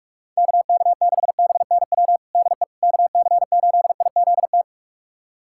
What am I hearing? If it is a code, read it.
Morse code, heard as a keyed tone, about 42 wpm, 700 Hz. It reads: KK6BNW DE KC8IZT
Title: USA Callsign DE Callsign 45wpm